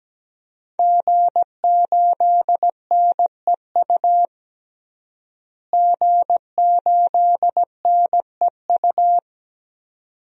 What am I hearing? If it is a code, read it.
Morse code, heard as a keyed tone, about 17 wpm, 700 Hz. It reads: G8NEU G8NEU
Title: DXCC-100 Call Signs - 5 Characters - Repeated 2x - 17wpm